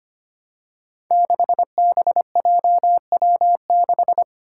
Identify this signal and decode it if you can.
Morse code, heard as a keyed tone, about 25 wpm, 700 Hz. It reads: 6BJW6
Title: Five Challenging Letter-Number Combinations 25wpm